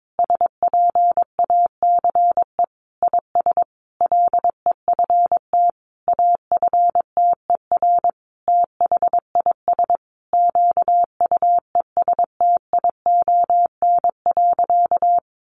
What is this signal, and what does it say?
Morse code, heard as a keyed tone, about 22 wpm, 700 Hz. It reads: SPACE IS LEFT AFTER THIS QUESTION.
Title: Sentences from Top 1500 Words 22wpm